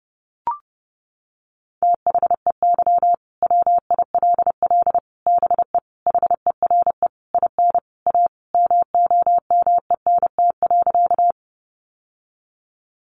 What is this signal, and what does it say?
Morse code, heard as a keyed tone, about 30 wpm, 700 Hz. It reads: THEY WILL BE HERE IN A MOMENT.
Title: Sentences from Top 1000 Words 30wpm